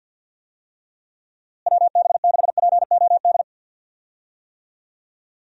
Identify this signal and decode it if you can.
Morse code, heard as a keyed tone, about 52 wpm, 700 Hz. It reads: WB6POD
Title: USA Call Signs - 6 Characters - Repeated 3x - 50wpm